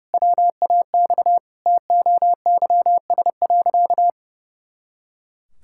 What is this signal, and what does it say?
Morse code, heard as a keyed tone, about 30 wpm, 700 Hz. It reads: WAX TOYS.
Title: Pangram Challenge v2 - 30wpm